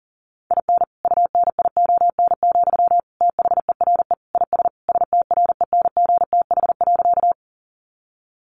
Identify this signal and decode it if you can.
Morse code, heard as a keyed tone, about 40 wpm, 700 Hz. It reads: IN UNION, THERE IS STRENGTH.
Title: Sentences from Top 1400 Words 40wpm